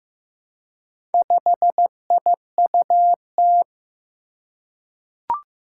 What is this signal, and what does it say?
Morse code, heard as a keyed tone, about 15 wpm, 700 Hz. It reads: 5IUT
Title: Four Challenging Letter-Number Combinations 15wpm